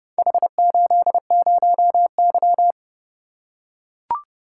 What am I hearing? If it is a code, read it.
Morse code, heard as a keyed tone, about 29 wpm, 700 Hz. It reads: H80Y